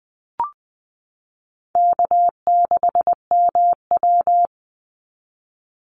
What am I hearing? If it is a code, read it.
Morse code, heard as a keyed tone, about 20 wpm, 700 Hz. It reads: K6MW
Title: NA SOTA Chaser Callsigns 20wpm